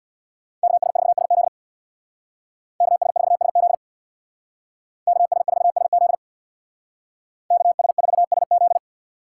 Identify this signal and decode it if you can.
Morse code, heard as a keyed tone, about 64 wpm, 700 Hz. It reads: KS4SZ KS4SZ KS4SZ KS4SZ